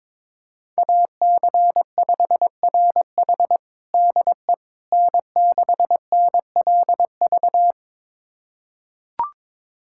Morse code, heard as a keyed tone, about 22 wpm, 700 Hz. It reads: AC5RH DE N6NLV
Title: USA Callsign DE Callsign 22wpm